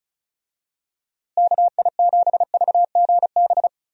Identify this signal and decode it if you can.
Morse code, heard as a keyed tone, about 36 wpm, 700 Hz. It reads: KI7VGB
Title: USA Call Signs - 6 Characters - Repeated 2x - 35wpm